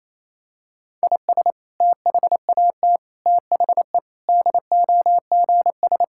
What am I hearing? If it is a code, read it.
Morse code, heard as a keyed tone, about 28 wpm, 700 Hz. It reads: IS THAT THE DOGS
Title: Sentences from Top 1600 Words 28wpm